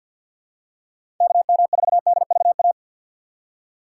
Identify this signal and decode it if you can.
Morse code, heard as a keyed tone, about 52 wpm, 700 Hz. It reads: KM4DVA
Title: USA Call Signs - 6 Characters - 4x Character Spacing - 50wpm